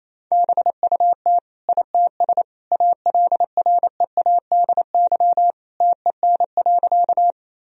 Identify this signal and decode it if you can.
Morse code, heard as a keyed tone, about 27 wpm, 700 Hz. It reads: BUT ITS ALREADY TEN.
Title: Sentences from Top 1000 Words - Review Part 2 - 28wpm